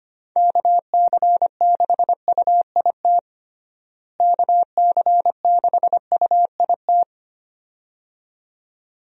Morse code, heard as a keyed tone, about 25 wpm, 700 Hz. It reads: KC6UIT KC6UIT
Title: USA Call Signs - 6 Characters - Repeated 2x - 25wpm